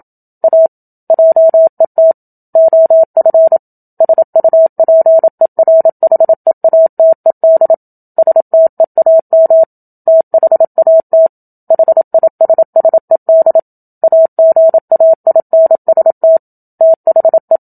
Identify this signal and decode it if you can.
Morse code, heard as a keyed tone, about 27 wpm, 650 Hz. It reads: A JET OF SUPERHEATED STEAM THAT HISSED AGAINST THE
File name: Book Two / 01-Chapter 1 - 27 WPM